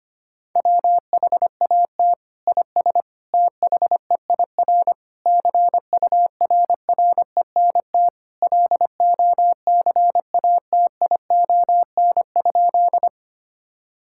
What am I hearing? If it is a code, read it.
Morse code, heard as a keyed tone, about 25 wpm, 700 Hz. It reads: WHAT IS THEIR CURRENT LOCATION?
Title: Sentences from Top 1000 Words - Review Part 6 - 25wpm